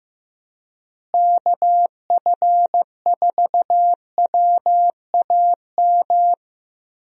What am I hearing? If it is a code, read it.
Morse code, heard as a keyed tone, about 15 wpm, 700 Hz. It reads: KF4WAM
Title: USA Call Signs - 6 Characters - Repeated 2x - 15wpm